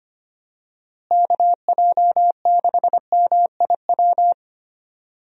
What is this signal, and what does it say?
Morse code, heard as a keyed tone, about 25 wpm, 700 Hz. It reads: KJ6MIW
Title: USA Call Signs - 6 Characters - Repeated 2x - 25wpm